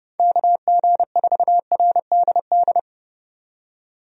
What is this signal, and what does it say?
Morse code, heard as a keyed tone, about 30 wpm, 700 Hz. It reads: KG4RDD
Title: USA Call Signs - 6 Characters - 30wpm